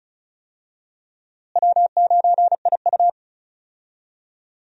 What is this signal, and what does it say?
Morse code, heard as a keyed tone, about 36 wpm, 700 Hz. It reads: W9IU
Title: USA Call Signs - 4 Characters - Repeated 2x - 35wpm